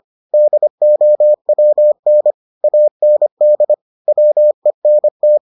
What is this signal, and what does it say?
Morse code, heard as a keyed tone, about 25 wpm, 600 Hz. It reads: DOWN AND WENT